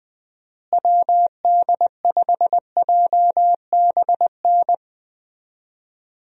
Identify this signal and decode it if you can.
Morse code, heard as a keyed tone, about 20 wpm, 700 Hz. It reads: WD5JBN
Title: USA Call Signs - 6 Characters - Repeated 2x - 20wpm